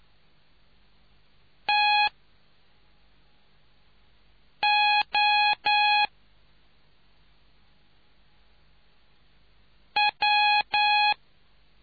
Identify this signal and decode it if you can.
Morse code, heard as a keyed tone, about 9 wpm, spaced as for 3 wpm, 830 Hz. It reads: TO W